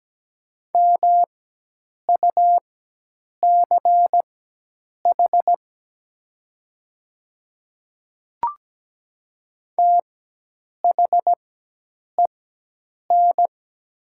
Morse code, heard as a keyed tone, about 17 wpm, spaced as for 8 wpm, 700 Hz. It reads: MUCH THEN
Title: 4x Character Spacing - Top 200 Words - 17wpm